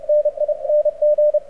Morse code, heard as a keyed tone, about 31 wpm, 590 Hz. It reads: NING
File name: archived / CRSnet